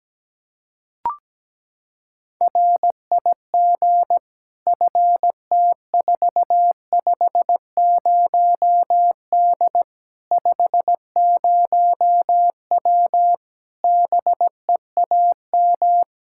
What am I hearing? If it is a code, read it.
Morse code, heard as a keyed tone, about 17 wpm, 700 Hz. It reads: RIG FT450D 50W BEAM
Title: Practice QSO - Exchange 2 - Info - 17wpm